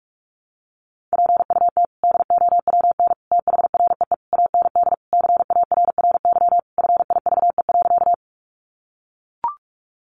Morse code, heard as a keyed tone, about 44 wpm, 700 Hz. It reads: PUT DOWN THREE AND CARRY FIVE.
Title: Sentences from Top 400 Words 45wpm